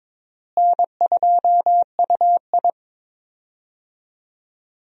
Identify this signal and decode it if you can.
Morse code, heard as a keyed tone, about 22 wpm, 700 Hz. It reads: N2UI
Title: USA Call Signs - 4 Characters - Repeated 3x - 22wpm